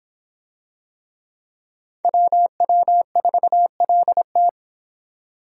Morse code, heard as a keyed tone, about 25 wpm, 700 Hz. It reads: WW4LT